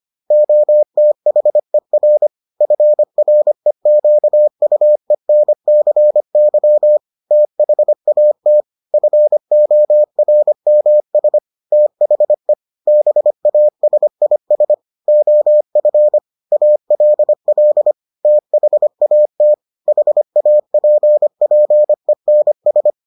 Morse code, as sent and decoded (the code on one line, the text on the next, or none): --- - .... .-. / ..-. .-. --.- ..- . -. -.-. -.-- / - .... .- - / ..-. --- .-. -- ... / - .... / -... .- ... .. ... / --- ..-. / .- .-.. .-.. / - .... .- - / .... .- .--. .--. -. ...
OTHER FREQUENCY THAT FORMS THE BASIS OF ALL THAT HAPPENS